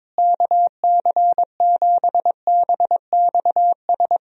Morse code, heard as a keyed tone, about 22 wpm, 700 Hz. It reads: KC7BXS
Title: USA Call Signs - 6 Characters - Repeated 3x - 22wpm